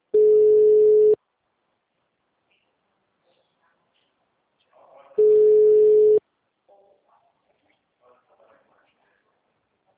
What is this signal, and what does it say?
Ringback tone (Europe, 425 Hz).